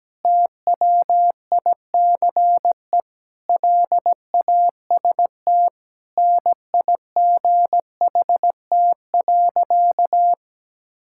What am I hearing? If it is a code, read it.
Morse code, heard as a keyed tone, about 17 wpm, 700 Hz. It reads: TWICE LAST NIGHT.
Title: Sentences from Top 1600 Words 17wpm